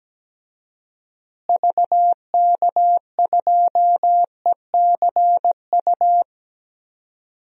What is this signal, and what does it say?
Morse code, heard as a keyed tone, about 17 wpm, 700 Hz. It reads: VK2ECU
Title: DXCC-100 Call Signs - 6 Characters - Repeated 3x - 17wpm